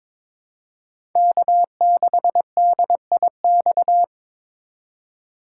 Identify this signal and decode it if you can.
Morse code, heard as a keyed tone, about 22 wpm, 700 Hz. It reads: K6DIX
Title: USA Call Signs - 5 Characters - 2x Character Spacing - 22wpm